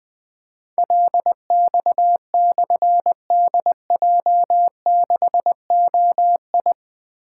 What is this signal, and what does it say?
Morse code, heard as a keyed tone, about 20 wpm, 700 Hz. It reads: LX/DJ6OI
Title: Difficult Call Signs 20wpm